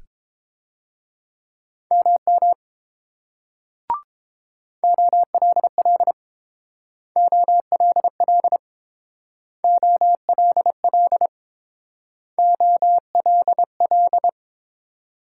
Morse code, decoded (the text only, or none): MM OLL OLL OLL OLL